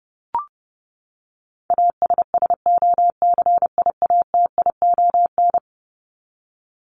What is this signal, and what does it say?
Morse code, heard as a keyed tone, about 30 wpm, 700 Hz. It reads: ASSOCIATION
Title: Top 1001-1250 Words 30wpm